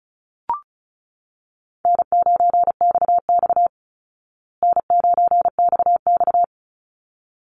Morse code, heard as a keyed tone, about 34 wpm, 700 Hz. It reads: N9XX N9XX
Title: Easy Call Signs Repeated 2x - 35wpm